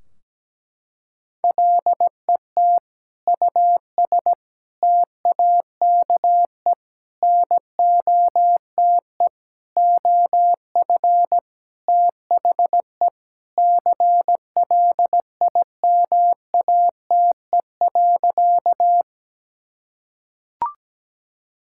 Morse code, heard as a keyed tone, about 17 wpm, 700 Hz. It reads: LET US TAKE NOTE OF THE CLIMATE.
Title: Sentences from Top 1600 Words 17wpm